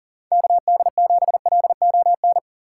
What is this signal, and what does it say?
Morse code, heard as a keyed tone, about 40 wpm, 700 Hz. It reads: KD7LON
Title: USA Call Signs - 6 Characters - 2x Character Spacing - 40wpm